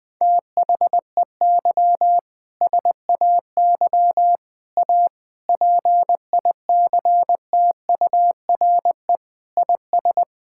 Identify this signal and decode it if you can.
Morse code, heard as a keyed tone, about 20 wpm, 700 Hz. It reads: THEY SAY A PICTURE IS
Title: Sentences with 7-Word Phrases - Spoken Beginning - 20wpm